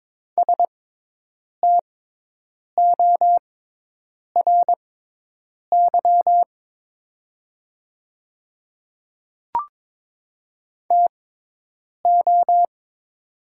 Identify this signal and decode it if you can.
Morse code, heard as a keyed tone, about 23 wpm, spaced as for 8 wpm, 700 Hz. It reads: STORY TO
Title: 6x Character Spacing - Top 500 Words - 22wpm